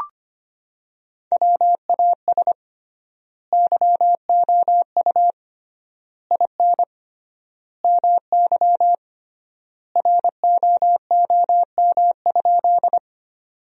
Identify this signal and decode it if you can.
Morse code, heard as a keyed tone, about 25 wpm, 700 Hz. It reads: WAS YOU IN MY ROOM?
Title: 3x Word-Spacing - Sentences from Top 300 Words - 25wpm